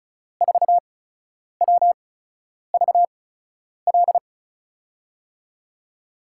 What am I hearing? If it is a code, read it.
Morse code, heard as a keyed tone, about 36 wpm, spaced as for 10 wpm, 700 Hz. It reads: 4WVL
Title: USA Call Signs - 6 Characters - 8x Character Spacing - 35wpm